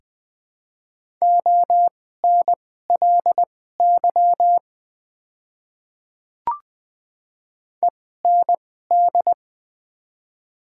Morse code, heard as a keyed tone, about 20 wpm, spaced as for 14 wpm, 700 Hz. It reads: ONLY END
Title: 2x Character Spacing - Top 200 Words - 20wpm